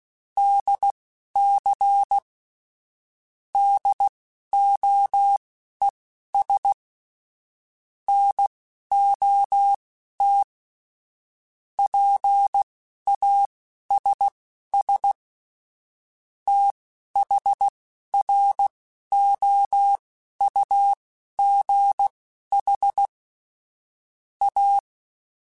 Morse code, heard as a keyed tone, about 16 wpm, spaced as for 11 wpm, 800 Hz. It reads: DC DOES NOT PASS THROUGH A
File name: Technical Subjects / Technical Subjects-mp3s / Sw Pwr Sup @10